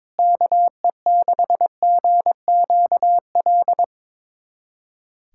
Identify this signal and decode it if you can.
Morse code, heard as a keyed tone, about 22 wpm, 700 Hz. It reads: KE6GQL
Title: USA Callsign DE Callsign 22wpm